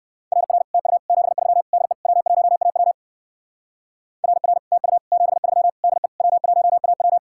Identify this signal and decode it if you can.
Morse code, heard as a keyed tone, about 56 wpm, 700 Hz. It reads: RR TU 73 DE W1AW RR TU 73 DE W1AW
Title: Speed-Racing Common QSO Phrases - 45wpm